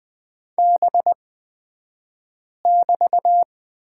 Morse code, heard as a keyed tone, about 20 wpm, spaced as for 5 wpm, 700 Hz. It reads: B<BT>
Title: 249 Lesson 40d - BK - Two Characters 20wpm